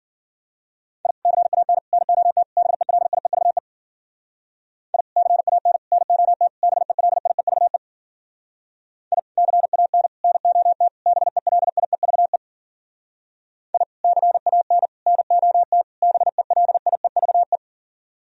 Morse code, heard as a keyed tone, about 51 wpm, 700 Hz. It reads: I CAN NOT BELIEVE I CAN NOT BELIEVE I CAN NOT BELIEVE I CAN NOT BELIEVE